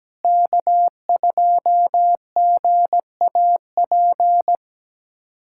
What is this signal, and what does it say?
Morse code, heard as a keyed tone, about 17 wpm, 700 Hz. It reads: K2GAP
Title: USA Callsign DE Callsign - 5 Characters - 17wpm